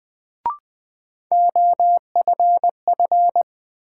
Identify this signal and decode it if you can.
Morse code, heard as a keyed tone, about 20 wpm, 700 Hz. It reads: OFF
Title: Sets of 2 Words - New Word Drill - 20wpm